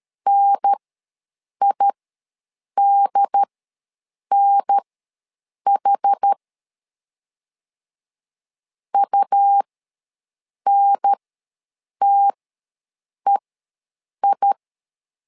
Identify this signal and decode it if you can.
Morse code, heard as a keyed tone, about 13 wpm, spaced as for 7 wpm, 800 Hz. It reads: NIDNH UNTEI
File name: LEZIONI / 2 UHDN / 2 UHDN ws 12wpm 1708